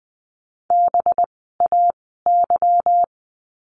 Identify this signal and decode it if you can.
Morse code, heard as a keyed tone, about 20 wpm, spaced as for 15 wpm, 700 Hz. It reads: BAY
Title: 185 Lesson 30f - Letter b - Words 20wpm